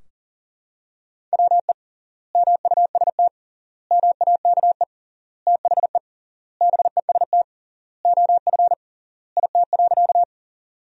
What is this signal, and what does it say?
Morse code, heard as a keyed tone, about 40 wpm, 700 Hz. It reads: WE MUST MAKE THE BEST OF IT.
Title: 3x Word-Spacing - Sentences from Top 400 Words - 40wpm